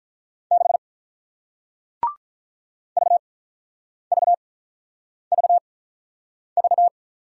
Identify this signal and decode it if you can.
Morse code, heard as a keyed tone, about 48 wpm, 700 Hz. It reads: 6 V V V V